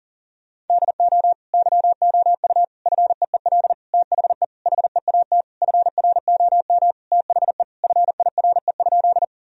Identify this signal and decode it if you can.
Morse code, heard as a keyed tone, about 37 wpm, 700 Hz. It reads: DO YOU FEEL THE HEAT FROM THE FIRE?
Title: Sentences from Top 1000 Words - Review Part 3 - 40wpm